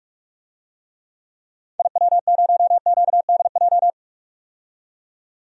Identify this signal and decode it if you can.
Morse code, heard as a keyed tone, about 43 wpm, 700 Hz. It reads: IW0QDJ
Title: DXCC-100 Call Signs - 6 Characters - 45wpm